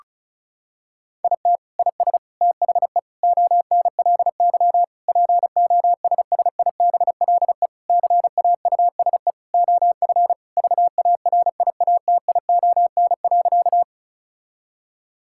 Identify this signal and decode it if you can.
Morse code, heard as a keyed tone, about 36 wpm, 700 Hz. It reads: IT IS THE ONLY POSSIBLE CAUSE OF VARIATION.
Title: Sentences from Top 1600 Words 35wpm